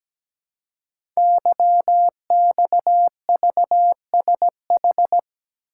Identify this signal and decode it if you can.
Morse code, heard as a keyed tone, about 17 wpm, 700 Hz. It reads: YXVSH
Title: Code Groups - 5 Letter:Numbers - 17wpm